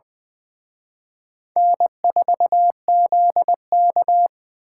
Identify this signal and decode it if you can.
Morse code, heard as a keyed tone, about 24 wpm, 700 Hz. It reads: N4ZK
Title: Speed Racing - USA 1x2 Call Signs 20wpm